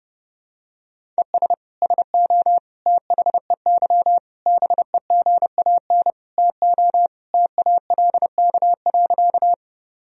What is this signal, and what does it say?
Morse code, heard as a keyed tone, about 30 wpm, 700 Hz. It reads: ES SO THEY BEGAN TO TALK.
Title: Sentences from Top 400 Words and Abbreviations 30wpm